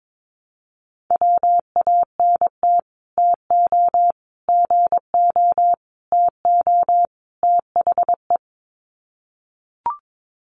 Morse code, heard as a keyed tone, about 22 wpm, 700 Hz. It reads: WANT TO GO TO THE
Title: Sets of 5 Words Encore2 22wpm